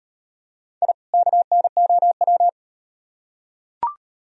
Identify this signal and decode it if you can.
Morse code, heard as a keyed tone, about 38 wpm, 700 Hz. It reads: I KNOW